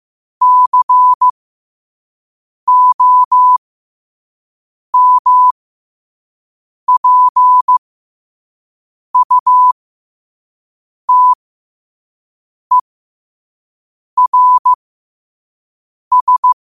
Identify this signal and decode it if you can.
Morse code, heard as a keyed tone, about 15 wpm, spaced as for 5 wpm, 1000 Hz. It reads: COMPUTERS
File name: qotdfiles / 5wpm